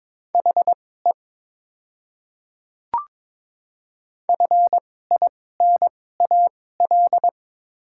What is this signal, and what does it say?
Morse code, heard as a keyed tone, about 22 wpm, spaced as for 16 wpm, 700 Hz. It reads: HE FINAL
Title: 2x Character Spacing - Top 500 Words - 22wpm